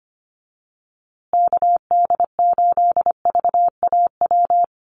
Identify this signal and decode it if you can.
Morse code, heard as a keyed tone, about 25 wpm, 700 Hz. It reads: KD8VAW
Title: USA Call Signs Repeated 3x 25wpm